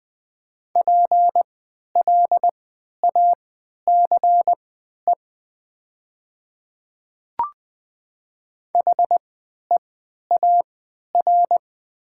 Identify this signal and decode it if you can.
Morse code, heard as a keyed tone, about 20 wpm, spaced as for 11 wpm, 700 Hz. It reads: PLACE HEAR
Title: 3x Character Spacing - Top 500 Words - 20wpm